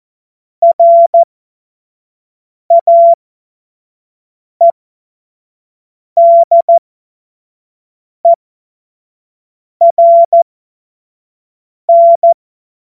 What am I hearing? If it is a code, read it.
Morse code, heard as a keyed tone, about 14 wpm, spaced as for 5 wpm, 680 Hz. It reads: RAEDERN
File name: Morsekurs / Witze / Witz 07. Dezember 2021, 25 F70